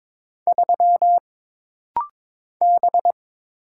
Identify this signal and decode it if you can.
Morse code, heard as a keyed tone, about 22 wpm, 700 Hz. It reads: 3 B
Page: This practice set is composed of Letters, Numbers, common punctuations, and common prosigns.